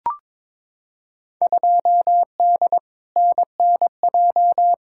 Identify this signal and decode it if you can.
Morse code, heard as a keyed tone, about 22 wpm, 700 Hz. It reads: 2D NNJ
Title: Field Day Exchanges - 22wpm